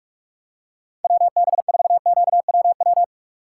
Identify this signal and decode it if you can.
Morse code, heard as a keyed tone, about 48 wpm, 700 Hz. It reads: WB4QWW